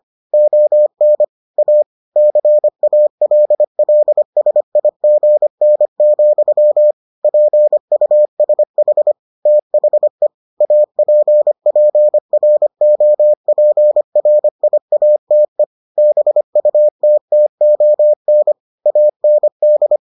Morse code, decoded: ON A CALLSIGN, PUSH THE APPROPRIATE BUTTON AND

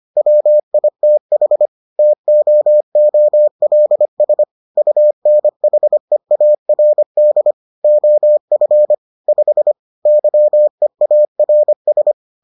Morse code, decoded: WITH TOOLS UNHEARD OF 5 YEARS

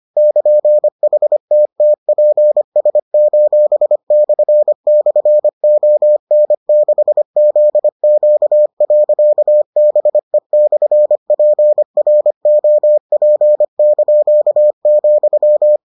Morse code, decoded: <KN>HTTPS://ON6ZQ.BE/PROP),

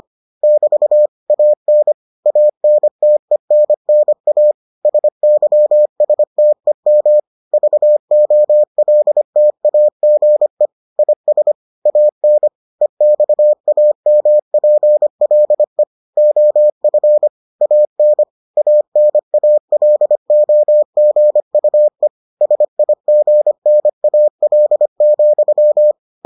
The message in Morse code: <BT> AN ANTENNA SYSTEM VOLTAGE IS AN EXAMPLE OF AN ANALOGUE SIGNAL,

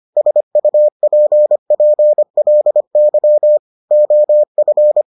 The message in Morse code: SUPPLY OF

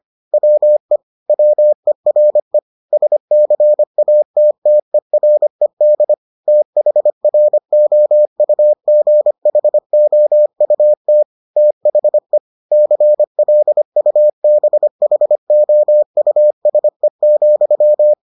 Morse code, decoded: WE WERE SCATTERED THROUGHOUT THE CLUBHOUSE,

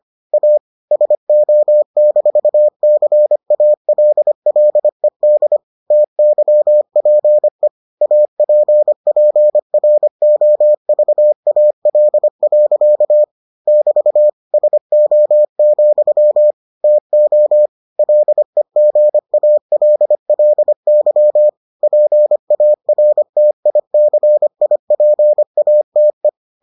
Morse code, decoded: A SO-CALLED TYPE APPROVAL. <BT> SO, TO LEGALLY PARTICIPATE

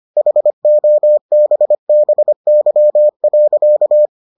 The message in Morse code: HOBBY.

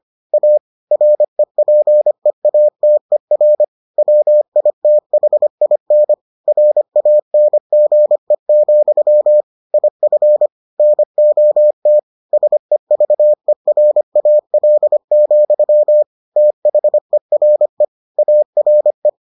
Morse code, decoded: A REPEATER WITHIN RANGE, IF NOT SEVERAL, THERE ARE

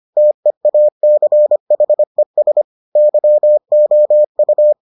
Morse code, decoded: TEACHES YOU

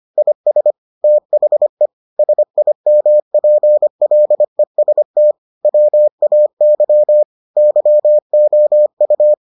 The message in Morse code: IS THE SIMPLEST WAY YOU

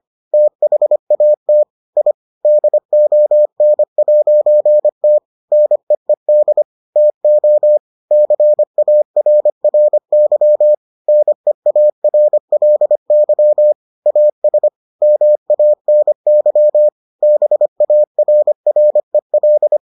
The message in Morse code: THAT I DON'T NEED TO CARRY NEARLY AS MANY BARREL

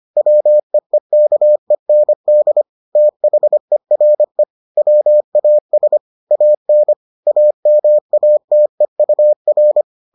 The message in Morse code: WEEKEND THERE WAS AN AMATEUR

